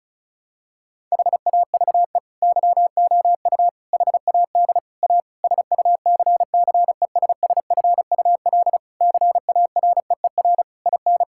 Morse code, heard as a keyed tone, about 36 wpm, 700 Hz. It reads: HAVE YOU HAD A SUCCESSFUL CAREER IN
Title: Sentences from Top 1000 Words - Review Part 6 - 35wpm